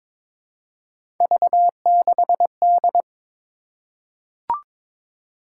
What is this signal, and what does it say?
Morse code, heard as a keyed tone, about 22 wpm, 700 Hz. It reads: V6D
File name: practice / 22wpm / Three Letter-Number Combinations - 22wpm